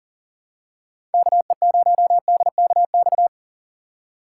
Morse code, heard as a keyed tone, about 40 wpm, 700 Hz. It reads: KE0DKX
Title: USA Call Signs - 6 Characters - 6x Character Spacing - 40wpm